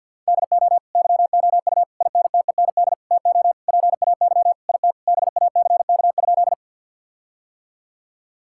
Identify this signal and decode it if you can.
Morse code, heard as a keyed tone, about 51 wpm, 700 Hz. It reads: DO YOU INTEND TO PAY IT BACK?
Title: Sentences from Top 1600 Words 50wpm